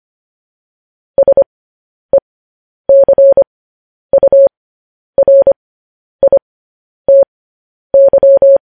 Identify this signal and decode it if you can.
Morse code, heard as a keyed tone, about 25 wpm, spaced as for 10 wpm, 550 Hz. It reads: SECURITY
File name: podcast / News headlines-1x-25x10